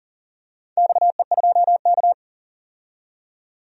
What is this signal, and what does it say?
Morse code, heard as a keyed tone, about 40 wpm, 700 Hz. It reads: XE2K
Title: DXCC-100 Call Signs - 4 Characters - 40wpm